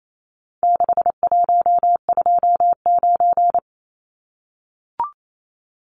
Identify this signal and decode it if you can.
Morse code, heard as a keyed tone, about 28 wpm, 700 Hz. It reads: 6129
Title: Numbers - 4-Digit 28wpm